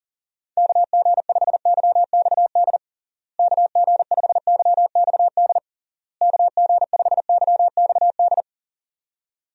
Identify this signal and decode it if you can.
Morse code, heard as a keyed tone, about 40 wpm, 700 Hz. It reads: KG5YXD KG5YXD KG5YXD
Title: USA Call Signs - 6 Characters - Repeated 3x - 40wpm